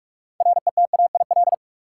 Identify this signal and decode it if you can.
Morse code, heard as a keyed tone, about 46 wpm, 700 Hz. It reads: RETAIL
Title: Practice QSO - Exchange 3 - More Info - 45wpm